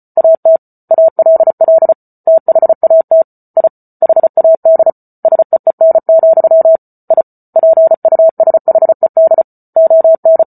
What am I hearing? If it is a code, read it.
Morse code, heard as a keyed tone, about 34 wpm, 650 Hz. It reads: AT ALL THAT I HAD SEEN, I PUSHED ON